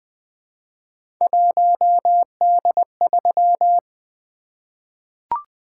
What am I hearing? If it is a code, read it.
Morse code, heard as a keyed tone, about 20 wpm, 700 Hz. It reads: 1D3
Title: Three Letter-Number Combinations - 20wpm